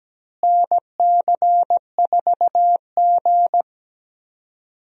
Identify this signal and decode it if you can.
Morse code, heard as a keyed tone, about 17 wpm, 700 Hz. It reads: NC4G
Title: USA Call Signs - 4 Characters - Repeated 2x - 17wpm